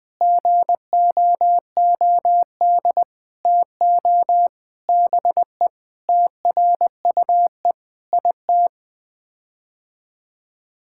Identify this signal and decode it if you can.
Morse code, heard as a keyed tone, about 20 wpm, 700 Hz. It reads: GOOD TO BE TRUE IT
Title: Sets of 9 Words - 20wpm